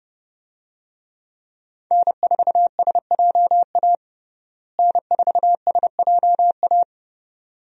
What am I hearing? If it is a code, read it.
Morse code, heard as a keyed tone, about 30 wpm, 700 Hz. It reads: N4SJA N4SJA
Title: USA Call Signs - 5 Characters - Repeated 2x - 30wpm